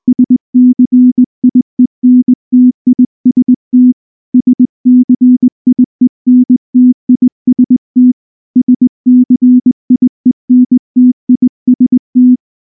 Morse code, heard as a keyed tone, about 19 wpm, 250 Hz. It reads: SCIENTIST SCIENTIST SCIENTIST
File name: morse / morse-training-20WPM-4